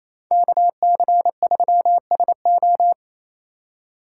Morse code, heard as a keyed tone, about 33 wpm, 700 Hz. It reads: KC3SO
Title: Speed Racing - USA 2x2 Call Signs 28wpm